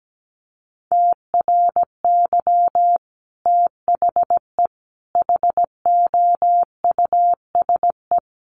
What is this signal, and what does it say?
Morse code, heard as a keyed tone, about 17 wpm, 700 Hz. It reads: TRY THE HOUSE
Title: Sentences from Top 1300 Words 17wpm